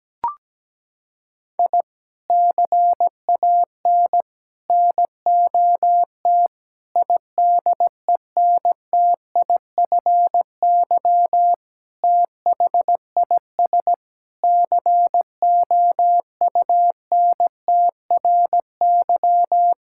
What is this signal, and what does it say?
Morse code, heard as a keyed tone, about 17 wpm, 700 Hz. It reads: I CAN NOT IDENTIFY THIS COUNTRY
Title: Sentences from Top 1000 Words - Review Part 6 - 17wpm